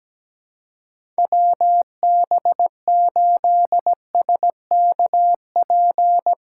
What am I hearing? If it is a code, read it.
Morse code, heard as a keyed tone, about 17 wpm, 700 Hz. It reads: WB8SKP
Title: K1USN SST QSO Practice - 17wpm